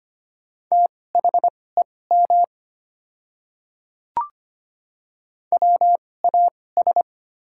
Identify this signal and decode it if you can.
Morse code, heard as a keyed tone, about 25 wpm, spaced as for 18 wpm, 700 Hz. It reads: THEM WAS
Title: 2x Character Spacing - Top 100 Words - 25wpm